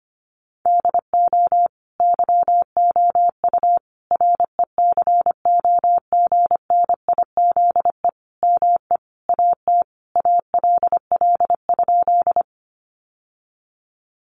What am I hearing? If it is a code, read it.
Morse code, heard as a keyed tone, about 25 wpm, 700 Hz. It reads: DO YOU RECOGNIZE ME AT ALL?
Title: Sentences from Top 1400 Words 25wpm